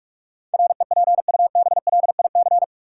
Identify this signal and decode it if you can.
Morse code, heard as a keyed tone, about 46 wpm, 700 Hz. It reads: REPUBLIC
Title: Sentences from Top 1500 Words - New Word Drill - 45wpm